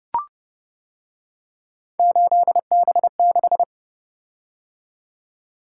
Morse code, heard as a keyed tone, about 29 wpm, 700 Hz. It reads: 8B6